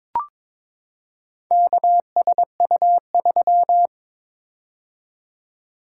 Morse code, heard as a keyed tone, about 22 wpm, 700 Hz. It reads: KSU3
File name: practice / 22wpm / Four Letter-Number Combinations - 22wpm